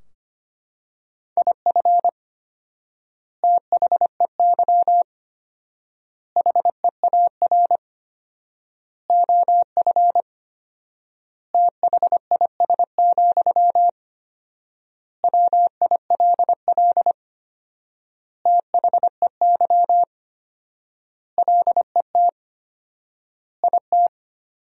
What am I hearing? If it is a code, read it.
Morse code, heard as a keyed tone, about 25 wpm, 700 Hz. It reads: IF THEY HEAR OF THIS, WILL THEY LET IT
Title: 4x Word-Spacing - Sentences from Top 400 Words - 25wpm